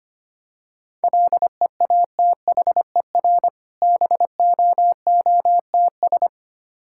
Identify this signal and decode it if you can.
Morse code, heard as a keyed tone, about 25 wpm, 700 Hz. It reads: LEATHER BOOTS
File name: practice / 25wpm / Difficult Words with Hints - 25wpm